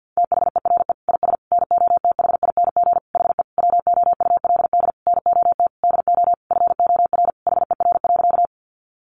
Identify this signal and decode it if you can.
Morse code, heard as a keyed tone, about 50 wpm, 700 Hz. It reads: THERE IS NOTHING HE WOULD NOT DO FOR HER.
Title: Sentences from Top 600 Words 50wpm